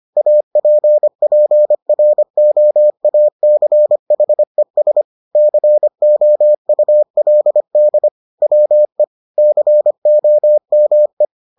Morse code, heard as a keyed tone, about 25 wpm, 600 Hz. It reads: APPROACHES COULD WE COME